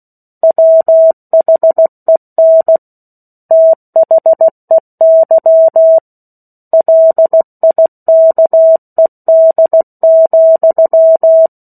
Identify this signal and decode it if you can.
Morse code, heard as a keyed tone, about 16 wpm, 650 Hz. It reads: WHEN THEY LIKED,
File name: 03-Chapter-03 16 WPM